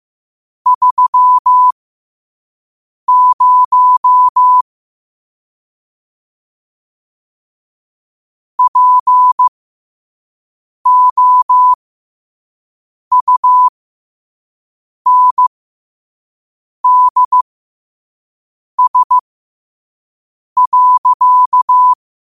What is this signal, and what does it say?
Morse code, heard as a keyed tone, about 15 wpm, spaced as for 5 wpm, 1000 Hz. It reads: 30 POUNDS.
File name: qotdfiles / 5wpm